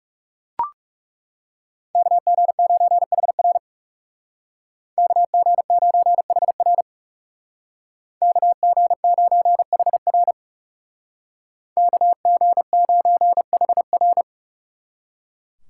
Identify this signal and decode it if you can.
Morse code, heard as a keyed tone, about 39 wpm, 700 Hz. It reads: KG9HR KG9HR KG9HR KG9HR